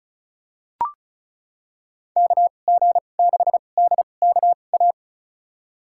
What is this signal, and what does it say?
Morse code, heard as a keyed tone, about 36 wpm, spaced as for 25 wpm, 700 Hz. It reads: KG6DKA
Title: USA Call Signs - 6 Characters - 2x Character Spacing - 35wpm